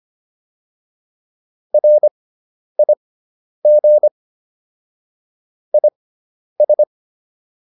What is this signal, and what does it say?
Morse code, heard as a keyed tone, about 25 wpm, spaced as for 10 wpm, 600 Hz. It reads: RIG IS
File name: QSO409 10